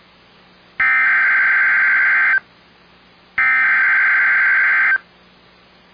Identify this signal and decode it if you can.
Emergency Alert System (SAME) tones (header bursts), heard as harsh buzzing bursts. It is heard from 0.8 to 4.9 s.